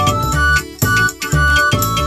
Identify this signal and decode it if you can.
There is no code.